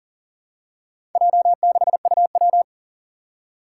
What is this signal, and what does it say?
Morse code, heard as a keyed tone, about 40 wpm, 700 Hz. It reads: J6UW
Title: Four Challenging Letter-Number Combinations 40wpm